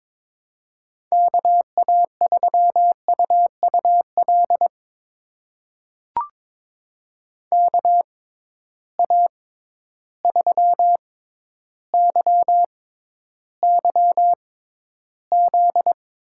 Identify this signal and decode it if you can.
Morse code, heard as a keyed tone, about 23 wpm, spaced as for 11 wpm, 700 Hz. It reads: KA3UUL KA3YYZ
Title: USA Call Signs - 6 Characters - 6x Character Spacing - 22wpm